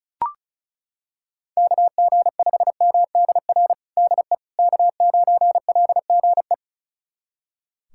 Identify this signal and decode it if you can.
Morse code, heard as a keyed tone, about 36 wpm, 700 Hz. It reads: KG5MDR DE K9LGE
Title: USA Callsign DE Callsign 35wpm